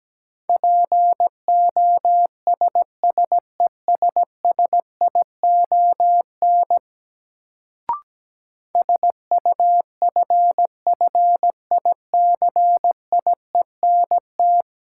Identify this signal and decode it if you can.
Morse code, heard as a keyed tone, about 17 wpm, 700 Hz. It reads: POSSESSION SUFFICIENT
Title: Sentences from Top 1400 Words - New Word Drill - 17wpm